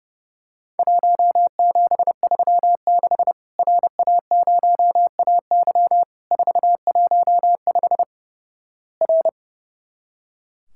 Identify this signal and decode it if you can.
Morse code, heard as a keyed tone, about 29 wpm, 700 Hz. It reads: 1736 RA0AY 415 R
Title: Worked All Europe DX Contest - QTC Exchange - 30wpm